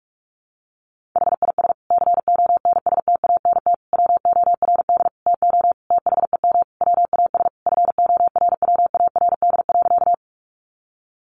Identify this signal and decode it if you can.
Morse code, heard as a keyed tone, about 44 wpm, 700 Hz. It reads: HIS CONSTANT WORD TO THEM WAS FORWARD.
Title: Sentences from Top 1500 Words 45wpm